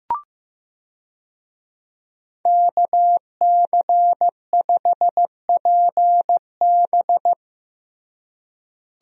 Morse code, heard as a keyed tone, about 15 wpm, 700 Hz. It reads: KC5PB
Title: USA Call Signs - 5 Characters - Repeated 3x - 15wpm